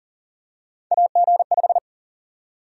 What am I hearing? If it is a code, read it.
Morse code, heard as a keyed tone, about 40 wpm, 700 Hz. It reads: AG5
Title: USA Call Sign Prefixes - Repeated 2x - 40wpm